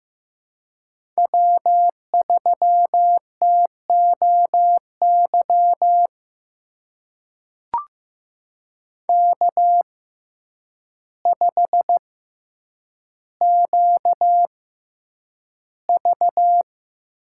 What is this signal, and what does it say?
Morse code, heard as a keyed tone, about 15 wpm, spaced as for 8 wpm, 700 Hz. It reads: W3TOY K5QV
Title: USA Call Signs - 6x Character Spacing - 15wpm